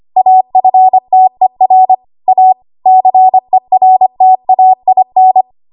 Morse code, heard as a keyed tone, about 25 wpm, 750 Hz. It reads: AFTER A CERTAIN
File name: Archive / 25 WPM / 25WPM